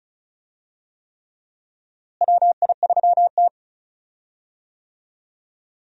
Morse code, heard as a keyed tone, about 36 wpm, 700 Hz. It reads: WI3T
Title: USA Call Signs - 4 Characters - Repeated 3x - 35wpm